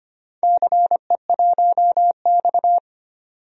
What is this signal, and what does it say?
Morse code, heard as a keyed tone, about 25 wpm, 700 Hz. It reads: CE1X
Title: DXCC-100 Call Signs - 4 Characters - 25wpm